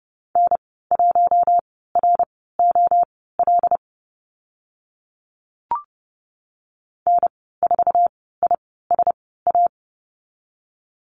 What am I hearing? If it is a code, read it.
Morse code, heard as a keyed tone, about 30 wpm, spaced as for 17 wpm, 700 Hz. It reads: N1ROL N4ISA
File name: Lesson 15g - Number 5 - Call Signs 30wpm